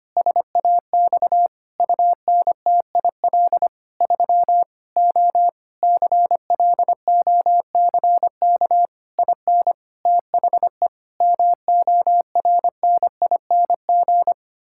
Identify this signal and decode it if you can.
Morse code, heard as a keyed tone, about 25 wpm, 700 Hz. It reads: SAX UNTIL 3 O CLOCK IN THE MORNING
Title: Pangram Challenge v2 - 25wpm